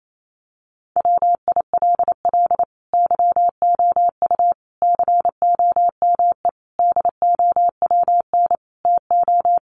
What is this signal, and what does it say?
Morse code, heard as a keyed tone, about 28 wpm, 700 Hz. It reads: WILL YOU COME DOWN TO
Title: Sentences from Top 1500 Words 28wpm